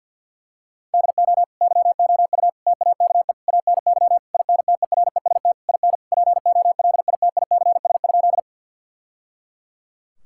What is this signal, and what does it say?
Morse code, heard as a keyed tone, about 50 wpm, 700 Hz. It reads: DO YOU TAKE ANY INTEREST IN POLITICS?
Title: Sentences from Top 1000 Words - Review Part 6 - 50wpm